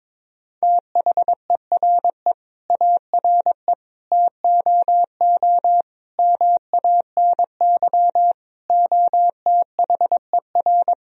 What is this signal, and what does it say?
Morse code, heard as a keyed tone, about 22 wpm, 700 Hz. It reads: THERE ARE TOO MANY OTHER
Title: Sentences from Top 1000 Words - Review Part 3 - 22wpm